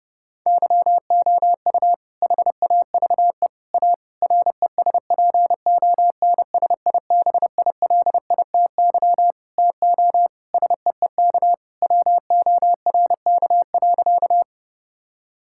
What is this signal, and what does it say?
Morse code, heard as a keyed tone, about 30 wpm, 700 Hz. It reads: YOU HAVE A RESPONSIBILITY TO SEEK WORK.
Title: Sentences from Top 1600 Words 30wpm